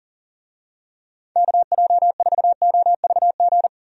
Morse code, heard as a keyed tone, about 40 wpm, 700 Hz. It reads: KJ4OVG
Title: USA Call Signs - 6 Characters - 40wpm